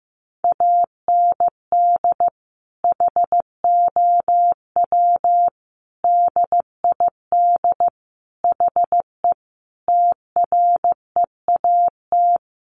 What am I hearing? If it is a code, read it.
Morse code, heard as a keyed tone, about 14 wpm, 700 Hz. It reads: AND HOW DID HE TREAT
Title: Sentences from Top 1300 Words 15wpm